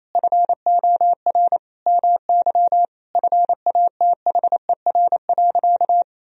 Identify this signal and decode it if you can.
Morse code, heard as a keyed tone, about 27 wpm, 700 Hz. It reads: FOR MY FATHER.
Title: Sentences from Top 1000 Words - Review Part 2 - 28wpm